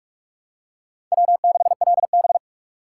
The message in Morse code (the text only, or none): W6LB